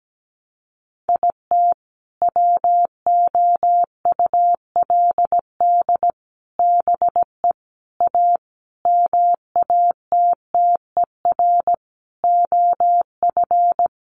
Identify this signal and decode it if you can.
Morse code, heard as a keyed tone, about 17 wpm, 700 Hz. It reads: IT WOULD BE A MATTER OF
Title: Sentences from Top 1400 Words 17wpm